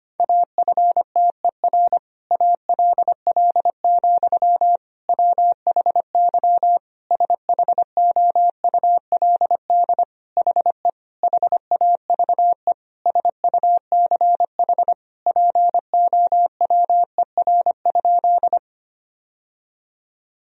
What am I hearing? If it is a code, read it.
Morse code, heard as a keyed tone, about 25 wpm, 700 Hz. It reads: AFTER ALL, WHY SHOULD HE HAVE SUCH POWER?
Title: Sentences from Top 1000 Words - Review Part 2 - 25wpm